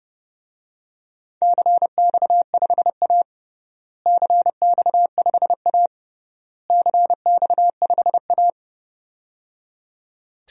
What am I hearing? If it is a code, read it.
Morse code, heard as a keyed tone, about 30 wpm, 700 Hz. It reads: CX5A CX5A CX5A
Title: DXCC-100 Call Signs - 4 Characters - Repeated 3x - 30wpm